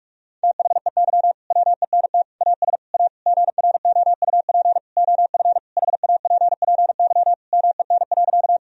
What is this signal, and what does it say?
Morse code, heard as a keyed tone, about 46 wpm, 700 Hz. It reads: THEY WENT AS A GROUP OF HAPPY MEN.
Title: Sentences from Top 1000 Words - Review Part 1 - 45wpm